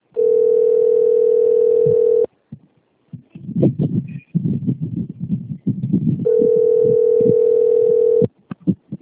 Ringback tone (North America), 440 + 480 Hz.
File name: rg-1600-unknown- - - .76863